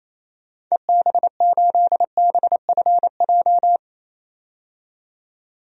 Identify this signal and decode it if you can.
Morse code, heard as a keyed tone, about 28 wpm, 700 Hz. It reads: EB8BFJ
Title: DXCC-100 Call Signs - 6 Characters - 28wpm